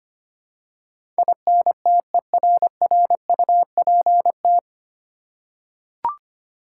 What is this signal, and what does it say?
Morse code, heard as a keyed tone, about 25 wpm, 700 Hz. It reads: INTERRUPT
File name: Nine Letter Words - 25wpm